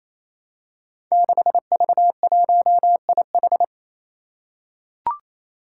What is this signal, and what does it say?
Morse code, heard as a keyed tone, about 27 wpm, 700 Hz. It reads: 6V1IH
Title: Five Challenging Letter-Number Combinations 28wpm